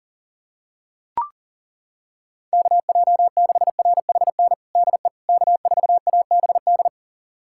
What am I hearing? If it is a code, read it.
Morse code, heard as a keyed tone, about 40 wpm, 700 Hz. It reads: KJ6RHN DE K4ABD
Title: USA Callsign DE Callsign 40wpm